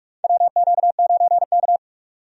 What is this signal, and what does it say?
Morse code, heard as a keyed tone, about 46 wpm, 700 Hz. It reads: WQ9K
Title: USA Call Signs - 4 Characters - 45wpm